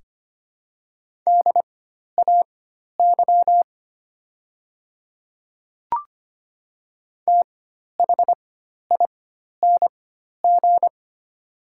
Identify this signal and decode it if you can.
Morse code, heard as a keyed tone, about 25 wpm, spaced as for 12 wpm, 690 Hz. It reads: DAY THING
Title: 4x Character Spacing - Top 200 Words - 25wpm